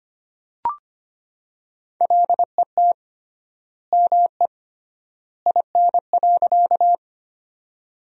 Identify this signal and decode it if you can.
Morse code, heard as a keyed tone, about 25 wpm, 700 Hz. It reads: LET ME IN.
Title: 3x Word-Spacing - Sentences from Top 300 Words - 25wpm